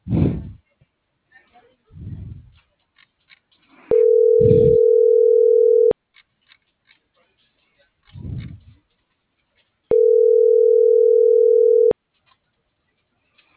Ringback tone (North America), 440 + 480 Hz.